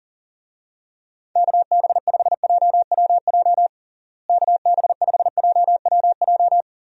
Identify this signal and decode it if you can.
Morse code, heard as a keyed tone, about 40 wpm, 700 Hz. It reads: KB5JWJ KB5JWJ
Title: USA Call Signs - 6 Characters - Repeated 2x - 40wpm